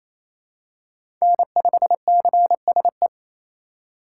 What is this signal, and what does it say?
Morse code, heard as a keyed tone, about 28 wpm, 700 Hz. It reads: N5CSE
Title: USA Call Signs - 5 Characters - 2x Character Spacing - 28wpm